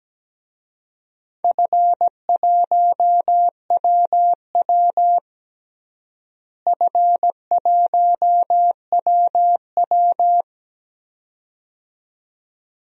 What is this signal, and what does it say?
Morse code, heard as a keyed tone, about 17 wpm, 700 Hz. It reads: F1WW F1WW
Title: DXCC-100 Call Signs - 4 Characters - Repeated 2x - 17wpm